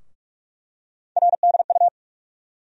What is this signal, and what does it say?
Morse code, heard as a keyed tone, about 47 wpm, 700 Hz. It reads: RDU
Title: USA Call Sign Postfixes - 3 Letters 45wpm